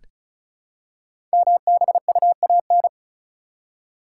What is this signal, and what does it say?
Morse code, heard as a keyed tone, about 36 wpm, 690 Hz. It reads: MBUAN
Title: Code Groups - 5 Letters - 35wpm